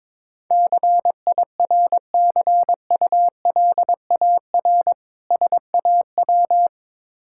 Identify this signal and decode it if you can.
Morse code, heard as a keyed tone, about 22 wpm, 690 Hz. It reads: CIRCULAR SAW
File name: Difficult Words with Hints - 22wpm